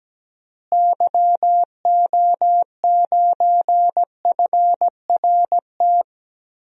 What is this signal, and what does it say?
Morse code, heard as a keyed tone, about 17 wpm, 700 Hz. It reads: YO9FRT
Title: DXCC-100 Call Signs - 6 Characters - Repeated 3x - 17wpm